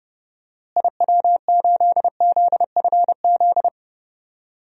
Morse code, heard as a keyed tone, about 28 wpm, 700 Hz. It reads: IW8ZFZ
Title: DXCC-100 Call Signs - 6 Characters - Repeated 3x - 30wpm